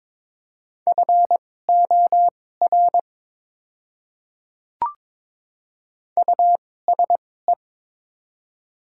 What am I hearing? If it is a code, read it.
Morse code, heard as a keyed tone, about 22 wpm, spaced as for 16 wpm, 700 Hz. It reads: FOR USE
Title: 2x Character Spacing - Top 100 Words - 22wpm